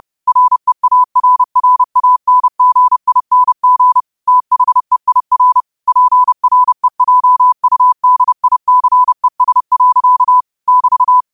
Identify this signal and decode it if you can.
Morse code, heard as a keyed tone, about 30 wpm, 1000 Hz. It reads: REARRANGING THEIR PREJUDICES. <BT>
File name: qotdfiles / 30wpm